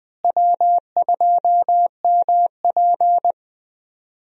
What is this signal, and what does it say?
Morse code, heard as a keyed tone, about 20 wpm, 700 Hz. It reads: W2MP
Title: USA Call Signs - 4 Characters - Repeated 3x - 20wpm